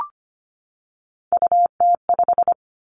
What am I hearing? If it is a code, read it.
Morse code, heard as a keyed tone, about 25 wpm, 700 Hz. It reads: UT5
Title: Callsign Prefixes for Top 25 DXCC - 25wpm